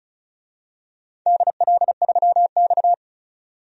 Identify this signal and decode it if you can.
Morse code, heard as a keyed tone, about 36 wpm, 700 Hz. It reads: DL3X
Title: DXCC-100 Call Signs - 4 Characters - 35wpm